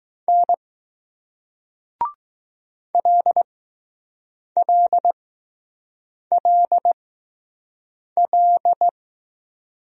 Morse code, heard as a keyed tone, about 20 wpm, spaced as for 6 wpm, 700 Hz. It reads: N LLLL